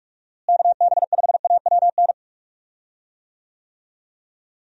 Morse code, heard as a keyed tone, about 46 wpm, 700 Hz. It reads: KB5AWN
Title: USA Call Signs - 6 Characters - Repeated 3x - 45wpm